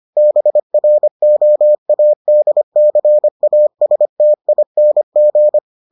Morse code, heard as a keyed tone, about 25 wpm, 600 Hz. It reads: BROADCASTING